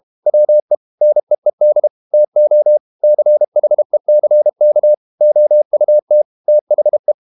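Morse code, heard as a keyed tone, about 32 wpm, 600 Hz. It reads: WE NEED TO CHECK OUT THE
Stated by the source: Morse code practice at 32 wpm from CW med Gnister, reading Ken Follett's novel Never in English.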